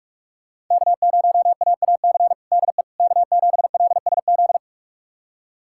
Morse code, heard as a keyed tone, about 47 wpm, 700 Hz. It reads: K0AAC DE K7LSZ
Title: USA Callsign DE Callsign - 5 Characters - 45wpm